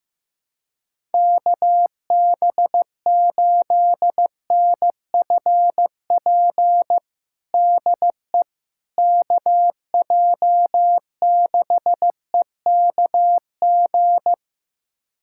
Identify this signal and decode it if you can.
Morse code, heard as a keyed tone, about 15 wpm, 700 Hz. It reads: KB8NFP DE KJ6EKG
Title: USA Callsign DE Callsign - 6 Characters - 15wpm